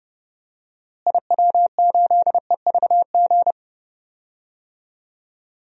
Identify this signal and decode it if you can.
Morse code, heard as a keyed tone, about 30 wpm, 700 Hz. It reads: IW8EVG
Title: DXCC-100 Call Signs - 6 Characters - Repeated 2x - 30wpm